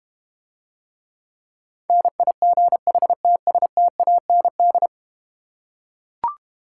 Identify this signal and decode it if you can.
Morse code, heard as a keyed tone, about 31 wpm, 700 Hz. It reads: NIGHTSTAND
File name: Easy Words - Speed-Racing with Hints - 25wpm